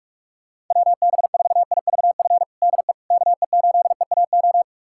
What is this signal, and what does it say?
Morse code, heard as a keyed tone, about 45 wpm, 700 Hz. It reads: WB4IVF DE KE8EAO
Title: USA Callsign DE Callsign 45wpm